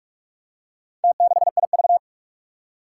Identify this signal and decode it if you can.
Morse code, heard as a keyed tone, about 46 wpm, 700 Hz. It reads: T6IV
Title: Four Challenging Letter-Number Combinations 45wpm